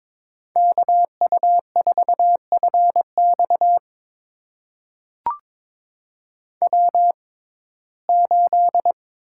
Morse code, heard as a keyed tone, about 22 wpm, spaced as for 16 wpm, 700 Hz. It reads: KU4FX W8
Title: USA Call Signs - 5 Characters - 6x Character Spacing - 22wpm